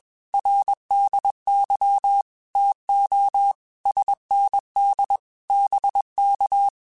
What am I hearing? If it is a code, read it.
Morse code, heard as a keyed tone, about 21 wpm, 800 Hz. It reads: RDY TO SND BK
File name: QSOs / QSOs-mp3s / WA7QOX 1@20